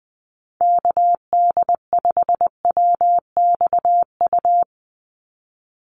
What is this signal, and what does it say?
Morse code, heard as a keyed tone, about 20 wpm, 700 Hz. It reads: KD5WXU
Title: USA Call Signs 20wpm